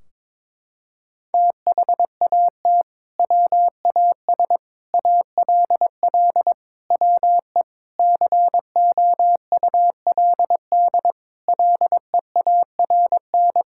Morse code, heard as a keyed tone, about 22 wpm, 700 Hz. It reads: THAT WAS ALL WE COULD LEARN
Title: Sentences from Top 1600 Words 22wpm